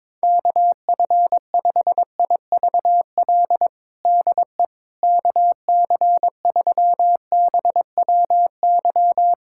Morse code, heard as a keyed tone, about 22 wpm, 700 Hz. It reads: KF5IVL DE KC3BWY
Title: USA Callsign DE Callsign - 6 Characters - 22wpm